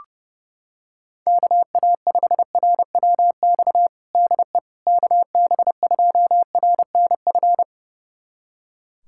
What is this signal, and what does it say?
Morse code, heard as a keyed tone, about 32 wpm, 700 Hz. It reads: KA5RWX DE KB2RNF
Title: USA Callsign DE Callsign - 6 Characters - 30wpm